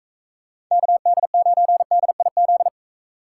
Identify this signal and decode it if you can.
Morse code, heard as a keyed tone, about 42 wpm, 700 Hz. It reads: KD9DIZ